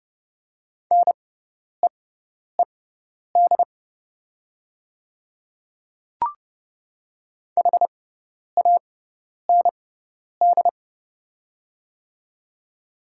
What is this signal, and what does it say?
Morse code, heard as a keyed tone, about 30 wpm, spaced as for 10 wpm, 700 Hz. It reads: NEED HAND
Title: 6x Character Spacing - Top 300 Words - 30wpm